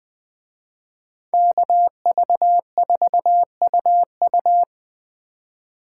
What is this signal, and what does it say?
Morse code, heard as a keyed tone, about 24 wpm, 700 Hz. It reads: KV4UU
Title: Speed Racing - USA 2x2 Call Signs 20wpm